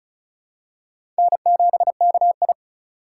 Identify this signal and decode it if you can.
Morse code, heard as a keyed tone, about 34 wpm, 700 Hz. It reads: N7KI
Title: USA Call Signs - 4 Characters - 6x Character Spacing - 35wpm